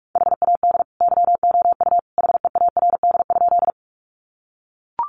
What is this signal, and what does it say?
Morse code, heard as a keyed tone, about 44 wpm, 700 Hz. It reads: HAD YOU HEARD?
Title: Sentences from Top 400 Words 45wpm